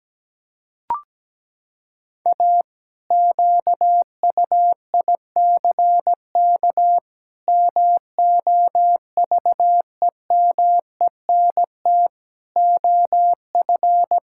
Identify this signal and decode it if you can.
Morse code, heard as a keyed tone, about 17 wpm, 700 Hz. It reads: A QUICK MOVEMENT OF
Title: Pangram Challenge v2 - 17wpm